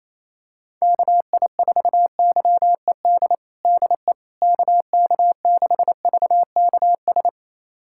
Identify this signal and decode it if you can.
Morse code, heard as a keyed tone, about 28 wpm, 700 Hz. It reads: KI4YED DE KK6VKS
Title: USA Callsign DE Callsign - 6 Characters - 28wpm